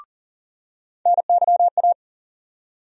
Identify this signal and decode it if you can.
Morse code, heard as a keyed tone, about 40 wpm, 700 Hz. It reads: NYA
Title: USA Call Sign Postfixes - 3 Letters 40wpm